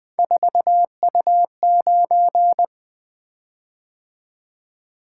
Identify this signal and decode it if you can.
Morse code, heard as a keyed tone, about 20 wpm, 700 Hz. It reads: 4U9
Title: Three Letter-Number Combinations - 20wpm